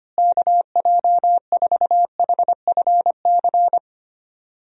Morse code, heard as a keyed tone, about 25 wpm, 700 Hz. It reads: KJ4HFC
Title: USA Call Signs - 6 Characters - Repeated 3x - 25wpm